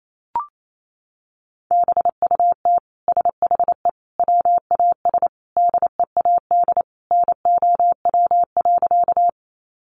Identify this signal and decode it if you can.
Morse code, heard as a keyed tone, about 28 wpm, 700 Hz. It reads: BUT SHE WAS DEAD NOW.
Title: Sentences from Top 1500 Words 28wpm